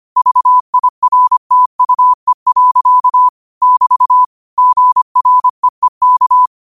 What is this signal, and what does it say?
Morse code, heard as a keyed tone, about 25 wpm, 1000 Hz. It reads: VIRTUE. <BT> GREEK